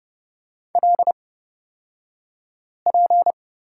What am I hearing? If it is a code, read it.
Morse code, heard as a keyed tone, about 31 wpm, spaced as for 5 wpm, 700 Hz. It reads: LP
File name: Lesson 36d - BT - Two Characters 30wpm